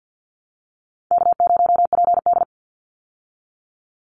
Morse code, heard as a keyed tone, about 50 wpm, 700 Hz. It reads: K0PD
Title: Easy Call Signs Repeated 3x - 50wpm